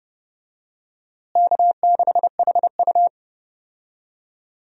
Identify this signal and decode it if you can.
Morse code, heard as a keyed tone, about 32 wpm, 700 Hz. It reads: K6HU
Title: USA Call Signs - 4 Characters - Repeated 3x - 30wpm